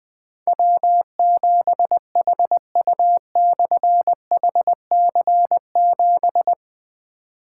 Morse code, heard as a keyed tone, about 20 wpm, 700 Hz. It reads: W7HU/HC7
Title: Difficult Call Signs 20wpm